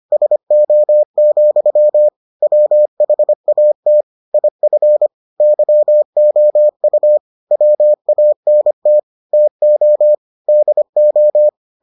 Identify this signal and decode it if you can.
Morse code, heard as a keyed tone, about 25 wpm, 600 Hz. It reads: SO, WHAT IF YOU WANT TO DO